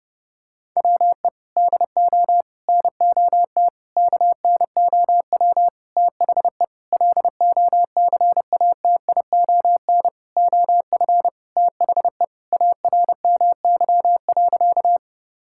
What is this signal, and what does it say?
Morse code, heard as a keyed tone, about 32 wpm, 700 Hz. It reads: WE DO NOT KNOW THE LOCATION OF THE ARMY.
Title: Sentences from Top 1000 Words - Review Part 6 - 30wpm